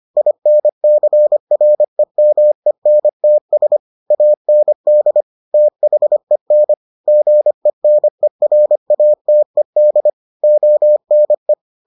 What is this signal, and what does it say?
Morse code, heard as a keyed tone, about 25 wpm, 600 Hz. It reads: INCREMENTS AND THEN GENERATED ONE